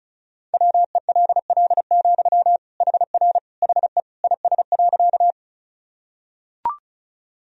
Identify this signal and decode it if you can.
Morse code, heard as a keyed tone, about 34 wpm, 690 Hz. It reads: WELL, HR HE IS.
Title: Sentences from Top 200 Words and Abbreviations 35wpm